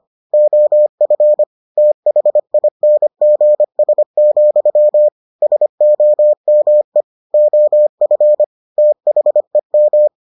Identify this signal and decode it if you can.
Morse code, heard as a keyed tone, about 25 wpm, 600 Hz. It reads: OF THINGS, SOME OF THEM